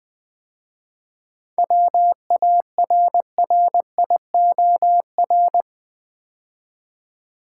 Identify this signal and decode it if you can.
Morse code, heard as a keyed tone, about 20 wpm, 700 Hz. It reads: WARRIOR
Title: 7 Letter Words - First 3 Letters Given - 20wpm